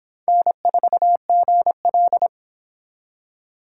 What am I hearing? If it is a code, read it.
Morse code, heard as a keyed tone, about 26 wpm, 700 Hz. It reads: N4GL